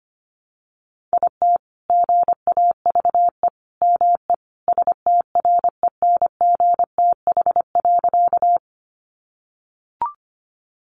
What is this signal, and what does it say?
Morse code, heard as a keyed tone, about 25 wpm, 700 Hz. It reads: IT GAVE ME STRENGTH.
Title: Sentences from Top 1300 Words 25wpm